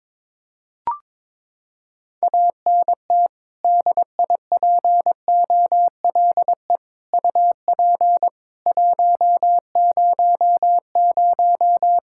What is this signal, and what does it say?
Morse code, heard as a keyed tone, about 21 wpm, 700 Hz. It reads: ANT DIPOLE UP 100
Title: Practice QSO - Exchange 2 - Info - 22wpm